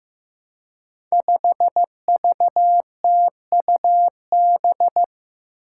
Morse code, heard as a keyed tone, about 15 wpm, 700 Hz. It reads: 5VTUB
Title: Five Challenging Letter-Number Combinations 15wpm